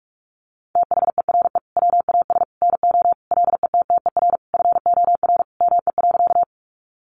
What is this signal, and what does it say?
Morse code, heard as a keyed tone, about 44 wpm, 700 Hz. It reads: THERE WAS NO LETTER FOR ME.
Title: Sentences from Top 300 Words 45wpm